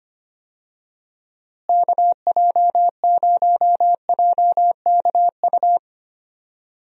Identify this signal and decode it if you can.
Morse code, heard as a keyed tone, about 25 wpm, 700 Hz. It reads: KJ0JKU
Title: USA Call Signs - 6 Characters - Repeated 2x - 25wpm